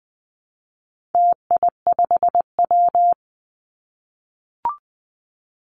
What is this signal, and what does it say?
Morse code, heard as a keyed tone, about 20 wpm, 700 Hz. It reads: TI5W
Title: Call Signs - 20wpm